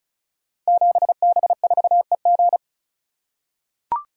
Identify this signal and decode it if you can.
Morse code, heard as a keyed tone, about 36 wpm, 700 Hz. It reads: 7B4EG
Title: Code Groups - 5 Letter:Numbers - 35wpm